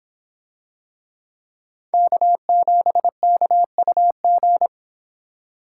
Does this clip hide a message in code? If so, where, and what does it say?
Yes: Morse code from 1.9 to 4.7 s, heard as a keyed tone, about 24 wpm, 700 Hz. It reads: K7KUG